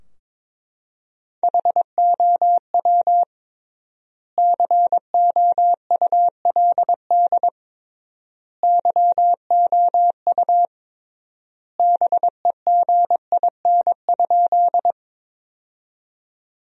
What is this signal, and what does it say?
Morse code, heard as a keyed tone, about 22 wpm, 700 Hz. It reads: HOW COULD YOU BEGIN?
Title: 3x Word-Spacing - Sentences from Top 300 Words - 22wpm